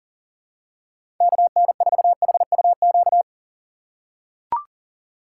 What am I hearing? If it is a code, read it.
Morse code, heard as a keyed tone, about 40 wpm, 700 Hz. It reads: KN4HUQ
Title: USA Call Signs - 6 Characters - 2x Character Spacing - 40wpm